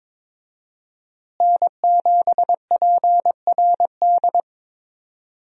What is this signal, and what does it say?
Morse code, heard as a keyed tone, about 21 wpm, 700 Hz. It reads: N7PRD
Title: USA Call Signs - 5 Characters - Repeated 2x - 22wpm